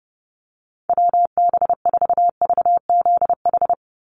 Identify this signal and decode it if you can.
Morse code, heard as a keyed tone, about 30 wpm, 700 Hz. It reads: WB4VZH
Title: USA Call Signs 30wpm